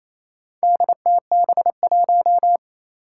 Morse code, heard as a keyed tone, about 28 wpm, 700 Hz. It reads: DTB1
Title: Four Challenging Letter-Number Combinations 28wpm